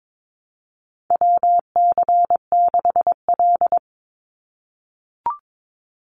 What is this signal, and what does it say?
Morse code, heard as a keyed tone, about 22 wpm, 700 Hz. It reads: WC6L
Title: USA Call Signs 22wpm